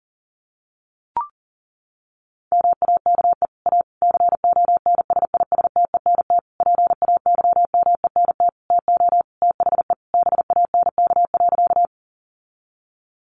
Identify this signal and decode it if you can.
Morse code, heard as a keyed tone, about 40 wpm, 700 Hz. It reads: MAKE A CONSISTENT PAYMENT TO THE BANK.
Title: Sentences from Top 1500 Words 40wpm